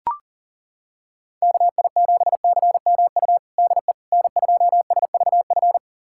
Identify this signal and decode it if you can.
Morse code, heard as a keyed tone, about 40 wpm, 700 Hz. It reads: KI7CMU DE N2SVF
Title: USA Callsign DE Callsign 40wpm